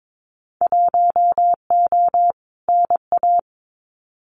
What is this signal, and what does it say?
Morse code, heard as a keyed tone, about 22 wpm, 700 Hz. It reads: 1O NA
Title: Winter Field Day Exchanges 22wpm